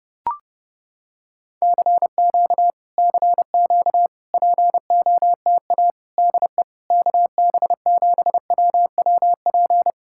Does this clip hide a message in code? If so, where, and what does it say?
Yes: Morse code from 1.6 to 9.9 s, heard as a keyed tone, about 30 wpm, 700 Hz. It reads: CQ CQ POTA DE KB7WWP